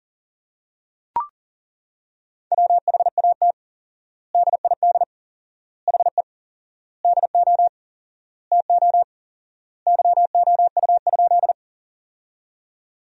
Morse code, heard as a keyed tone, about 37 wpm, 700 Hz. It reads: WHAT DID HE DO TO YOU?
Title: 4x Word-Spacing - Sentences from Top 100 Words - 40wpm